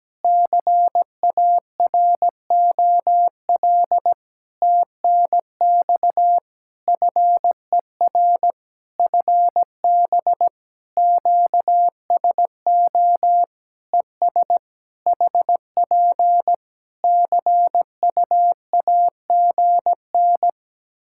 Morse code, heard as a keyed tone, about 17 wpm, 700 Hz. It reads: CAROL TNX FER FB QSO ES HP CUAGN
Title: Practice QSO - Exchange 4 - Ending - 17wpm